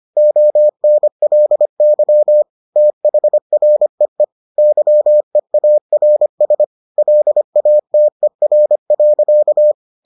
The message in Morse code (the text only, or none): ONLY THREE YEARS LATER.